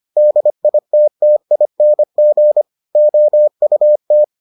Morse code, heard as a keyed tone, about 25 wpm, 600 Hz. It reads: DITTING OUT